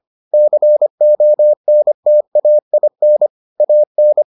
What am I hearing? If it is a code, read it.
Morse code, heard as a keyed tone, about 25 wpm, 600 Hz. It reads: CONTAIN AN